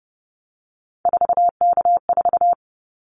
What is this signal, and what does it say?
Morse code, heard as a keyed tone, about 30 wpm, 700 Hz. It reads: 4K4
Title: Callsign Prefixes for Top 100 DXCC - 30wpm